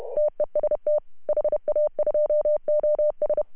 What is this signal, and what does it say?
Morse code, heard as a keyed tone, about 33 wpm, 600 Hz. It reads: TEST HA2OS
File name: CW / HA2OS RHOB 2017 CW